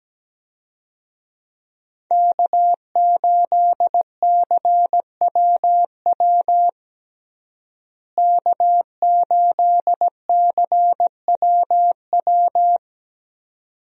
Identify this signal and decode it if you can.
Morse code, heard as a keyed tone, about 17 wpm, 700 Hz. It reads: K8CWW K8CWW
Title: USA Call Signs - 5 Characters - Repeated 2x - 17wpm